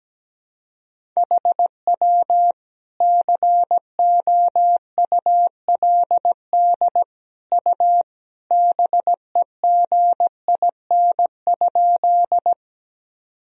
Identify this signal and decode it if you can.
Morse code, heard as a keyed tone, about 17 wpm, 700 Hz. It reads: HW COULD U BEGIN?
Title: Sentences from Top 300 Words and Abbreviations 17wpm